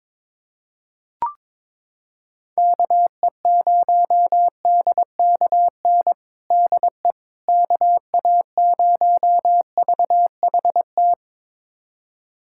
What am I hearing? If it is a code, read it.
Morse code, heard as a keyed tone, about 22 wpm, 700 Hz. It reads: KE0DKN DE KA0VHT
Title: USA Callsign DE Callsign - 6 Characters - 22wpm